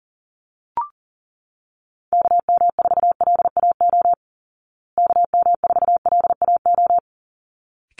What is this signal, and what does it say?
Morse code, heard as a keyed tone, about 40 wpm, 700 Hz. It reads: KM4LAO KM4LAO
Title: Call Signs Repeated 3x - 40wpm